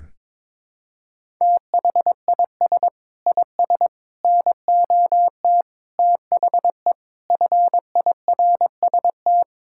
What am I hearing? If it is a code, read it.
Morse code, heard as a keyed tone, about 21 wpm, 700 Hz. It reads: THIS IS NOT THE FIRST
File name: practice / 22wpm / Sets of 6 Words - 22wpm